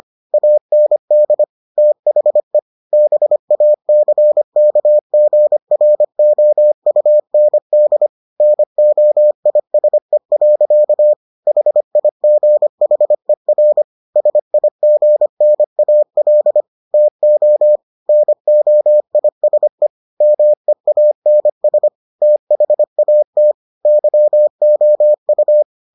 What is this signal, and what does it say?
Morse code, heard as a keyed tone, about 25 wpm, 600 Hz. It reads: AND THE BACKGROUND NOISE. HIGHER SIGNAL TO NOISE MEANS THAT YOU